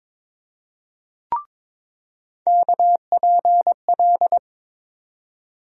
Morse code, heard as a keyed tone, about 22 wpm, 700 Hz. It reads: KPL
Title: USA Call Sign Postfixes - 3 Letters 22wpm